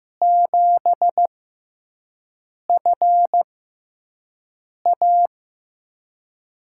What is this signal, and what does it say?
Morse code, heard as a keyed tone, about 15 wpm, spaced as for 5 wpm, 700 Hz. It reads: 7FA
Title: USA Call Signs - 5 Characters - 6x Character Spacing - 15wpm